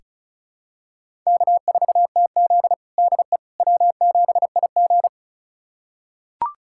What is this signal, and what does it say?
Morse code, heard as a keyed tone, about 34 wpm, 700 Hz. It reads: K4TZ DE W7IG
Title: USA Callsign DE Callsign - 4 Characters - 35wpm